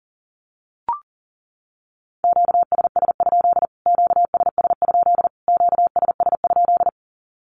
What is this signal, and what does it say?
Morse code, heard as a keyed tone, about 37 wpm, 700 Hz. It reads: QSS? QSS? QSS?
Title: Q-Codes Repeated 3x 40wpm